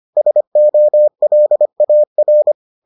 Morse code, heard as a keyed tone, about 25 wpm, 600 Hz. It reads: SOLAR